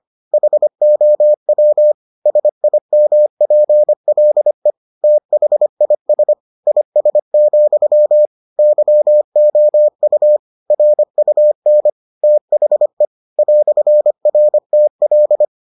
Morse code, heard as a keyed tone, about 25 wpm, 600 Hz. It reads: HOW SIMPLE THIS IS, YOU RUN THE "RTL